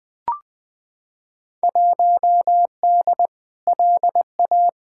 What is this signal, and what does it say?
Morse code, heard as a keyed tone, about 20 wpm, 700 Hz. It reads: 1D LA
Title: Field Day Exchanges - 20wpm